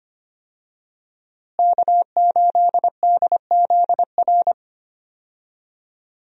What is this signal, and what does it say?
Morse code, heard as a keyed tone, about 25 wpm, 700 Hz. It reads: K8DZR
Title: USA Call Signs - 5 Characters - Repeated 2x - 25wpm